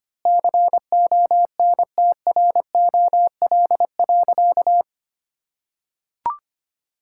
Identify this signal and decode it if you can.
Morse code, heard as a keyed tone, about 25 wpm, 700 Hz. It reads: CONTROL.
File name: practice / 25wpm / Sentences with 9-Word Phrases - 25wpm